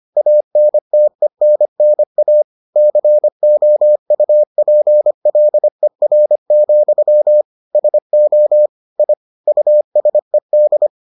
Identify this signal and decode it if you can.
Morse code, heard as a keyed tone, about 25 wpm, 600 Hz. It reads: ANTENNA COUPLER, SO I USED